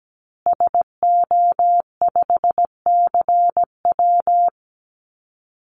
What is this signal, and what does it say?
Morse code, heard as a keyed tone, about 17 wpm, 700 Hz. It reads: SO5CW
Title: Call Signs Repeated 2x - 17wpm